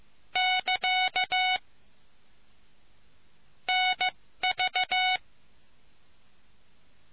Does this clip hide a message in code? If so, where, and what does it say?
Yes: Morse code from 0.4 to 5.2 s, heard as a keyed tone, about 15 wpm, spaced as for 13 wpm, 2950 Hz. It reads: <KA> NV